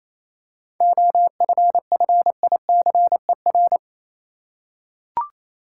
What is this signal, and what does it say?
Morse code, heard as a keyed tone, about 28 wpm, 700 Hz. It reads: OFFICER
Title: Seven Letter Words - 28wpm